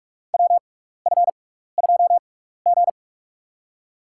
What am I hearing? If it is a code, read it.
Morse code, heard as a keyed tone, about 44 wpm, spaced as for 15 wpm, 700 Hz. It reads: WF2G
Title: USA Call Signs - 4 Characters - 6x Character Spacing - 45wpm